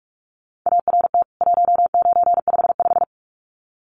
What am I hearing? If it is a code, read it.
Morse code, heard as a keyed tone, about 44 wpm, 700 Hz. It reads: ART 1955
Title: CWT Exchange 45wpm